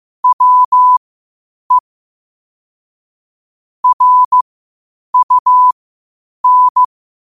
Morse code, heard as a keyed tone, about 15 wpm, spaced as for 8 wpm, 1000 Hz. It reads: WE RUN
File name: qotdfiles / 8wpm